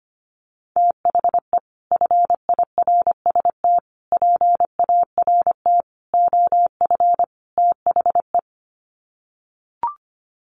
Morse code, heard as a keyed tone, about 25 wpm, 700 Hz. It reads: THE FIRST PART OF THE
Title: Sets of 5 Words Encore 25wpm